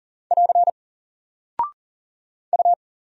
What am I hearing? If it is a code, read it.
Morse code, heard as a keyed tone, about 39 wpm, 700 Hz. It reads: <AR> U
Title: Single Character (Letter:Number:Punctuation:Prosign) 40wpm